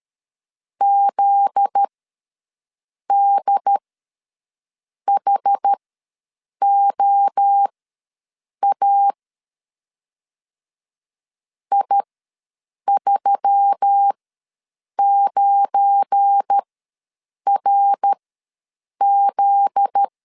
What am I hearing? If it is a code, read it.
Morse code, heard as a keyed tone, about 13 wpm, spaced as for 7 wpm, 800 Hz. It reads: ZDHOA I39RZ
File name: LEZIONI / 7 GF9Z / 7 GF9Z ws 12wpm 3321